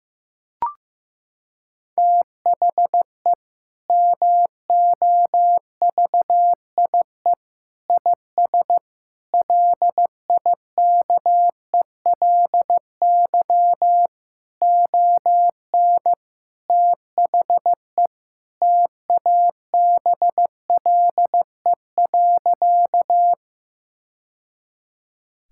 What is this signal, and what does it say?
Morse code, heard as a keyed tone, about 15 wpm, 700 Hz. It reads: THE MOVIE IS LIKELY ON THE TABLE.
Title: Sentences from Top 1000 Words - Review Part 5 - 15wpm